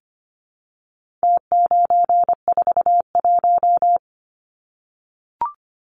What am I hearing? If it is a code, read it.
Morse code, heard as a keyed tone, about 25 wpm, 700 Hz. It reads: T941
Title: Numbers - Contest Style - 4-Digit 25wpm